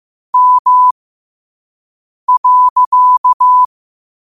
Morse code, heard as a keyed tone, about 15 wpm, spaced as for 5 wpm, 1000 Hz. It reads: M.
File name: qotdfiles / 5wpm